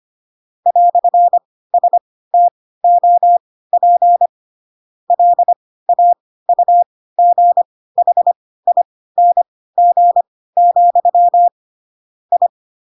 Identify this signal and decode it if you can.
Morse code, heard as a keyed tone, about 25 wpm, spaced as for 16 wpm, 700 Hz. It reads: "STOP LAUGHING, I
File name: Pinocchio / Pinocchio ch3 10 25 ik1hge